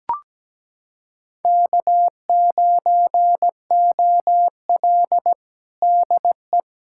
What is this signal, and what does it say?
Morse code, heard as a keyed tone, about 17 wpm, 700 Hz. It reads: K9OL DE
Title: USA Callsign DE Callsign - 4 Characters - 17wpm